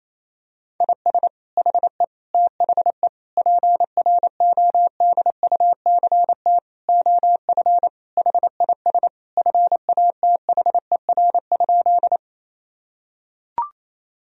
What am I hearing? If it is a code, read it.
Morse code, heard as a keyed tone, about 28 wpm, 700 Hz. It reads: IS HE THE PRODUCT OF HIS FATHER?
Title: Sentences from Top 1000 Words - Review Part 2 - 28wpm